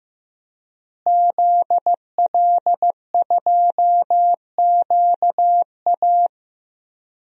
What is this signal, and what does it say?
Morse code, heard as a keyed tone, about 15 wpm, 700 Hz. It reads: ZL2QA
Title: DXCC-100 Call Signs - 5 Characters - Repeated 3x - 15wpm